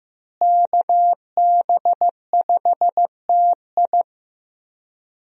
Morse code, heard as a keyed tone, about 15 wpm, 700 Hz. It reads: KB5TI
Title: USA Call Signs - 5 Characters - Repeated 2x - 15wpm